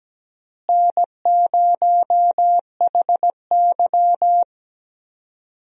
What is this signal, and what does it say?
Morse code, heard as a keyed tone, about 17 wpm, 700 Hz. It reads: N0HY
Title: USA Call Signs - 4 Characters - Repeated 3x - 17wpm